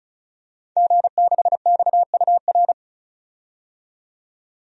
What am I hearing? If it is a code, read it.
Morse code, heard as a keyed tone, about 36 wpm, 700 Hz. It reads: G6XUR
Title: DXCC-100 Call Signs - 5 Characters - 35wpm